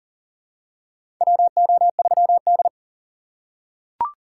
Morse code, heard as a keyed tone, about 37 wpm, 700 Hz. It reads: WO3D
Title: USA Call Signs - 4x Character Spacing - 40wpm